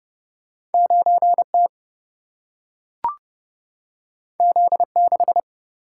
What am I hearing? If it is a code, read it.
Morse code, heard as a keyed tone, about 31 wpm, 700 Hz. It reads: 9 T Z 6
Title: Single-Letter Single-Number 30wpm